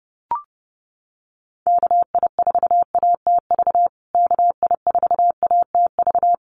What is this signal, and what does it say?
Morse code, heard as a keyed tone, about 30 wpm, 700 Hz. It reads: KI4ATV KI4ATV
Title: USA Call Signs Repeated 3x 30wpm